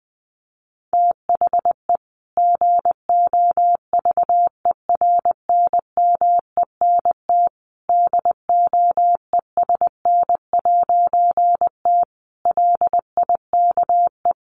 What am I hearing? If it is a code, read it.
Morse code, heard as a keyed tone, about 19 wpm, 700 Hz. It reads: THE GOVERNMENT DOESN'T LIKE
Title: A Sound Of Thunder 20wpm